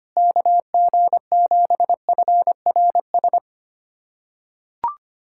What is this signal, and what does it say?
Morse code, heard as a keyed tone, about 25 wpm, 700 Hz. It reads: KG7FRS
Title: USA Callsign DE Callsign 25wpm